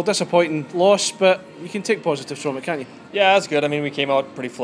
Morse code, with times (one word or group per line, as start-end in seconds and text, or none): none